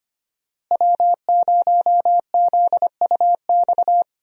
Morse code, heard as a keyed tone, about 25 wpm, 690 Hz. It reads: W0ZUX
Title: USA Call Signs - 5 Characters - Repeated 2x - 25wpm